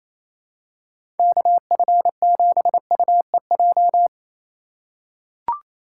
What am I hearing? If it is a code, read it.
Morse code, heard as a keyed tone, about 27 wpm, 700 Hz. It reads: KF7UEJ
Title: USA Call Signs - 6 Characters - 2x Character Spacing - 28wpm